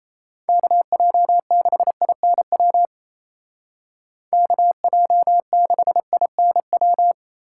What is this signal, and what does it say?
Morse code, heard as a keyed tone, about 32 wpm, 700 Hz. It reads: KJ6INW KJ6INW
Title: Speed Racing - USA 2x3 Call Signs 28wpm